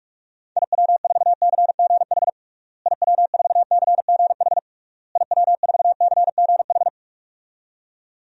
Morse code, heard as a keyed tone, about 46 wpm, 700 Hz. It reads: IW4CGH IW4CGH IW4CGH
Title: DXCC-100 Call Signs - 6 Characters - Repeated 3x - 45wpm